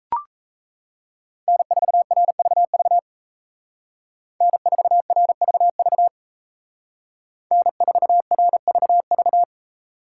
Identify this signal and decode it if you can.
Morse code, heard as a keyed tone, about 36 wpm, 700 Hz. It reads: N4RVV N4RVV N4RVV